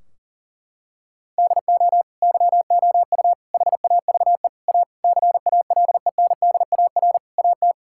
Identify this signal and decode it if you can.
Morse code, heard as a keyed tone, about 40 wpm, 700 Hz. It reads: DO YOU HAVE A CALENDAR AT
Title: Sentences from Top 1600 Words 40wpm